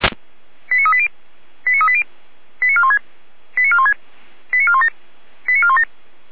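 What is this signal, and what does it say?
Five-tone selective call (ZVEI), heard as quick runs of five tones, tones of about 70 ms. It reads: 89380 89380 89416 89416 89417 89417
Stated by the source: Five-tone ZVEI selective call received by SDR.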